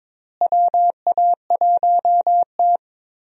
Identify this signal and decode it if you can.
Morse code, heard as a keyed tone, about 22 wpm, 700 Hz. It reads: WA1T
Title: USA Call Signs - 4 Characters - Repeated 3x - 22wpm